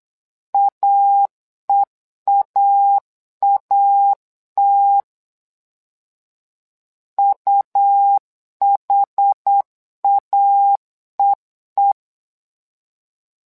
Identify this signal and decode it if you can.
Morse code, heard as a keyed tone, about 8 wpm, 800 Hz. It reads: AEAAT UHAEE
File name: LEZIONI / 2 UHDN / 2 UH 8wpm 2129